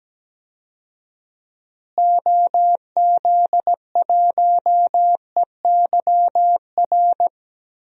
Morse code, heard as a keyed tone, about 17 wpm, 700 Hz. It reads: OZ1EYR
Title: DXCC-100 Call Signs - 6 Characters - Repeated 2x - 17wpm